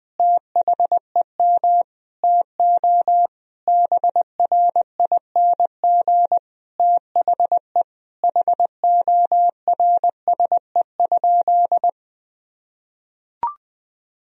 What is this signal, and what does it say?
Morse code, heard as a keyed tone, about 20 wpm, 700 Hz. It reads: THEM TO BRING THE HORSE?
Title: Sentences from Top 500 Words and Abbreviations 20wpm